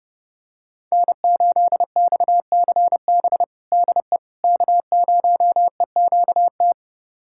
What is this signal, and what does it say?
Morse code, heard as a keyed tone, about 30 wpm, 700 Hz. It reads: N8XCB DE K0EQT
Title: USA Callsign DE Callsign - 5 Characters - 30wpm